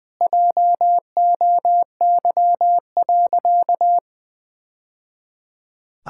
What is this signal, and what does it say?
Morse code, heard as a keyed tone, about 20 wpm, 700 Hz. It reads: JOY.
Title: Sentences with 3-Word Phrases - Spoken Beginning - 20wpm